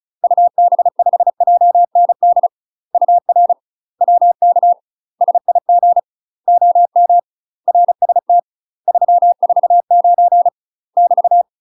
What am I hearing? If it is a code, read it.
Morse code, heard as a keyed tone, about 34 wpm, 700 Hz. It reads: UB5JND UR WK SIG OM RST 349 <BT>
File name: UK MASTER QSOs TRAINING/35WPM